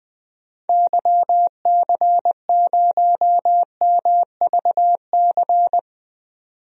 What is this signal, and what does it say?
Morse code, heard as a keyed tone, about 20 wpm, 700 Hz. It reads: YC0MVC
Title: DXCC-100 Call Signs - 6 Characters - 20wpm